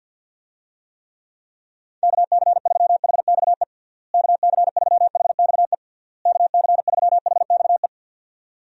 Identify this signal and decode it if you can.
Morse code, heard as a keyed tone, about 51 wpm, 700 Hz. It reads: KC3HXE KC3HXE KC3HXE
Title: USA Call Signs - 6 Characters - Repeated 3x - 50wpm